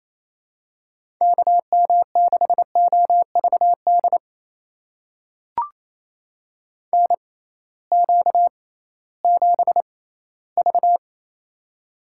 Morse code, heard as a keyed tone, about 27 wpm, spaced as for 16 wpm, 700 Hz. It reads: KM6OVD NQ7V
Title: USA Call Signs - 6x Character Spacing - 28wpm